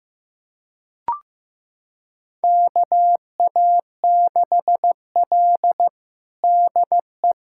Morse code, heard as a keyed tone, about 15 wpm, 700 Hz. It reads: KA6L DE
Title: USA Callsign DE Callsign - 4 Characters - 15wpm